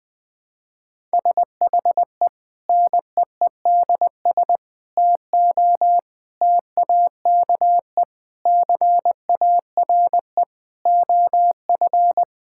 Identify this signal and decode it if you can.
Morse code, heard as a keyed tone, about 20 wpm, 700 Hz. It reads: SHE NEEDS TO TAKE CARE OF
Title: Sentences with 4-Word Phrases - 20wpm